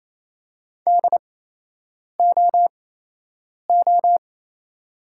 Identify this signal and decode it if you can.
Morse code, heard as a keyed tone, about 28 wpm, spaced as for 8 wpm, 690 Hz. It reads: DOO
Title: 8x Character Spacing - Top 300 Words - 28wpm